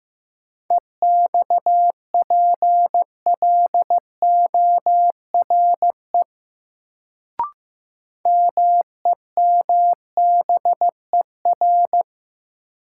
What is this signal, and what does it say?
Morse code, heard as a keyed tone, about 15 wpm, 700 Hz. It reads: EXPLORE MEMBER
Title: Sentences from Top 1100 Words - New Word Drill - 15wpm